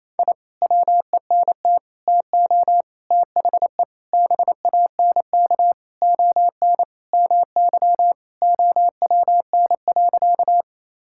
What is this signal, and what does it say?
Morse code, heard as a keyed tone, about 28 wpm, 700 Hz. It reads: I WENT TO THE BANK ON MY OWN.
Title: Sentences from Top 1000 Words - Review Part 6 - 28wpm